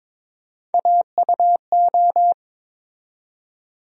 Morse code, heard as a keyed tone, about 23 wpm, 700 Hz. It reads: AUO
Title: USA Call Sign Postfixes - 3 Letters 22wpm